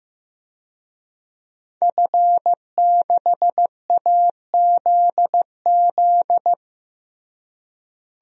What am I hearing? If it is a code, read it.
Morse code, heard as a keyed tone, about 15 wpm, 700 Hz. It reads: F6AZZ
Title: DXCC-100 Call Signs - 5 Characters - Repeated 3x - 15wpm